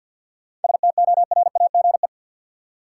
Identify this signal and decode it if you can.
Morse code, heard as a keyed tone, about 47 wpm, 700 Hz. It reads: STORAGE
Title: Seven Letter Words - 50wpm